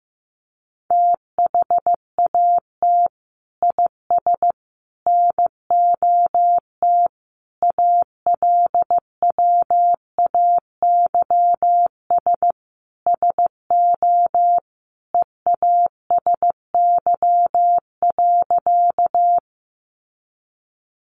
Morse code, heard as a keyed tone, about 15 wpm, 700 Hz. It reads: THAT IS NOT ALWAYS SO EASY.
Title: Sentences from Top 700 Words 15wpm